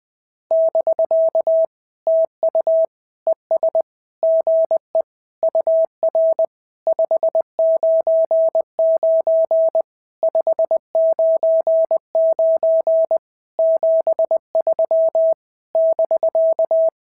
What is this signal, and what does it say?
Morse code, heard as a keyed tone, about 20 wpm, 650 Hz. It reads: <BK> TU ES GE UR 599 599 73 <BK>
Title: SOTA QSO Practice 20wpm